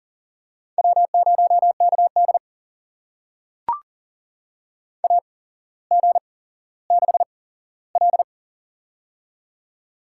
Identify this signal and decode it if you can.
Morse code, heard as a keyed tone, about 40 wpm, spaced as for 17 wpm, 700 Hz. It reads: W0KD AG6L
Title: USA Call Signs - 8x Character Spacing - 40wpm